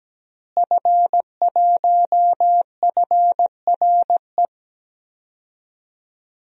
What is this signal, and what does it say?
Morse code, heard as a keyed tone, about 17 wpm, 700 Hz. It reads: F1FRE
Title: DXCC-100 Call Signs - 5 Characters - Repeated 3x - 17wpm